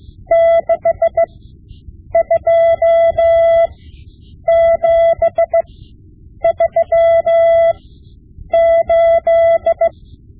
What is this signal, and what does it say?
Morse code, heard as a keyed tone, about 16 wpm, spaced as for 8 wpm, 670 Hz. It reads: 62738
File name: Morse Lessons / Morse Lessons / Tape-5B